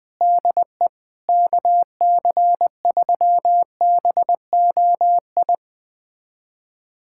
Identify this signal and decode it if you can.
Morse code, heard as a keyed tone, about 20 wpm, 700 Hz. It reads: DE KC3BOI
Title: USA Callsign DE Callsign 20wpm